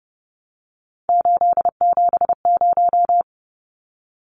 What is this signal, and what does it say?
Morse code, heard as a keyed tone, about 30 wpm, 700 Hz. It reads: 870
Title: Numbers - 3-Digit 30wpm